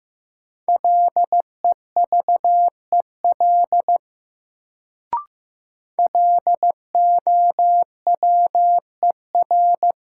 Morse code, heard as a keyed tone, about 15 wpm, 700 Hz. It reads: LEVEL LOWER
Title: Sentences from Top 700 Words - New Word Drill - 15wpm